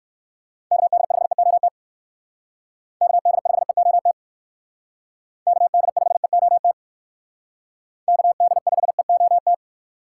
Morse code, heard as a keyed tone, about 58 wpm, 700 Hz. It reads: KD5EOT KD5EOT KD5EOT KD5EOT